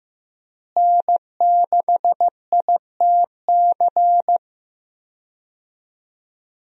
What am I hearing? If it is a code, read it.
Morse code, heard as a keyed tone, about 16 wpm, 700 Hz. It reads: N6ITC
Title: USA Call Signs - 5 Characters - Repeated 2x - 15wpm